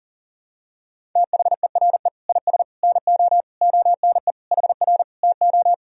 Morse code, heard as a keyed tone, about 40 wpm, 700 Hz. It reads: THERE IS NO ONE HR TO
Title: Sentences from Top 200 Words and Abbreviations 40wpm